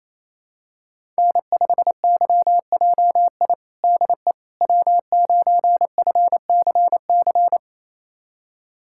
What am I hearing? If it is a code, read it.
Morse code, heard as a keyed tone, about 27 wpm, 700 Hz. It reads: N5YJI DE W9FCC
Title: USA Callsign DE Callsign - 5 Characters - 28wpm